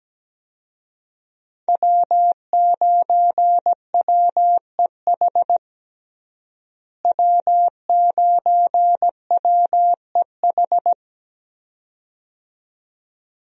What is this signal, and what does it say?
Morse code, heard as a keyed tone, about 17 wpm, 700 Hz. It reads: W9WEH W9WEH
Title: USA Call Signs - 5 Characters - Repeated 2x - 17wpm